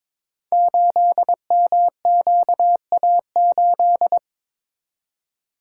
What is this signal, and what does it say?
Morse code, heard as a keyed tone, about 22 wpm, 700 Hz. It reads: 8MQA8
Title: Code Groups - 5 Letter:Numbers - 22wpm